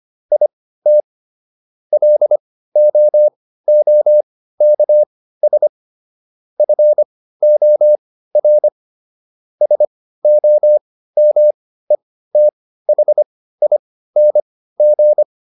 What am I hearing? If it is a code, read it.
Morse code, heard as a keyed tone, about 25 wpm, spaced as for 15 wpm, 600 Hz. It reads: IT LOOKS FOR SOMETHING